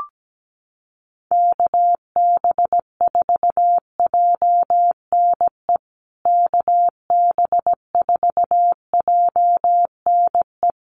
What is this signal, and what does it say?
Morse code, heard as a keyed tone, about 17 wpm, 700 Hz. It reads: KB4JNE KB4JNE
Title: USA Call Signs Repeated 3x 17wpm